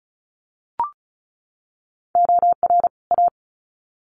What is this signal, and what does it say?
Morse code, heard as a keyed tone, about 34 wpm, 700 Hz. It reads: OR A
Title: Sets of 2 Words 35wpm